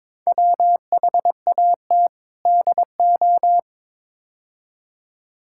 Morse code, heard as a keyed tone, about 22 wpm, 700 Hz. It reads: WHAT DO
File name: practice / 22wpm / Build A Phrase - 5-Word Phrases - 22wpm